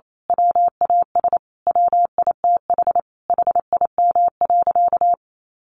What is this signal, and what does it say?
Morse code, heard as a keyed tone, about 28 wpm, 700 Hz. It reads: WAS WITH HIM.
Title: Sentences from Top 1500 Words 28wpm